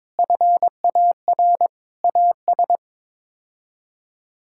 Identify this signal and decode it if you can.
Morse code, heard as a keyed tone, about 22 wpm, 700 Hz. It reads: FAR AS
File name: practice / 22wpm / Build A Phrase - 5-Word Phrases - 22wpm